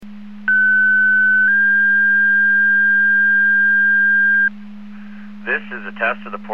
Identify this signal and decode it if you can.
Two-tone page, heard as a one-second tone, then a three-second tone, about 1500 then 1600 Hz, with speech. It is heard from 0.5 to 4.5 s.